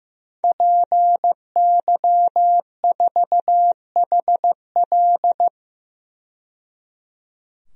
Morse code, heard as a keyed tone, about 15 wpm, 700 Hz. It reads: PY4HL
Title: DXCC-100 Call Signs - 5 Characters - 15wpm